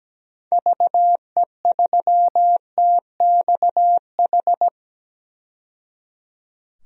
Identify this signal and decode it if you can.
Morse code, heard as a keyed tone, about 17 wpm, 700 Hz. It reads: VE3TXH
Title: DXCC-100 Call Signs - 6 Characters - 17wpm